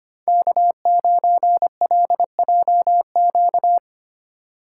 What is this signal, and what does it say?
Morse code, heard as a keyed tone, about 25 wpm, 700 Hz. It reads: K9LJQ
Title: USA Call Signs - 5 Characters - 3x Character Spacing - 25wpm